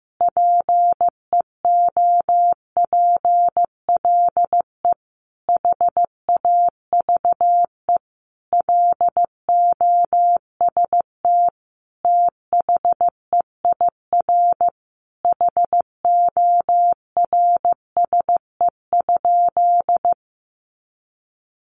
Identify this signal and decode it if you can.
Morse code, heard as a keyed tone, about 15 wpm, 700 Hz. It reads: PEOPLE HAVE LOST THEIR HORSE?
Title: Sentences from Top 1400 Words 15wpm